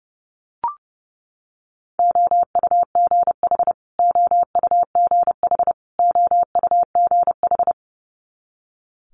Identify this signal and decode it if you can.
Morse code, heard as a keyed tone, about 30 wpm, 700 Hz. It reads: OUGH OUGH OUGH
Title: Multi-letter Phonemes Repeated 3x 30wpm